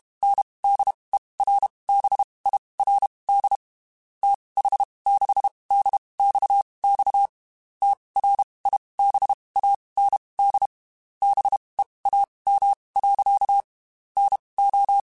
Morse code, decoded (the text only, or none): NDERBIRD TH6DXX TRIBAND BEAM. NO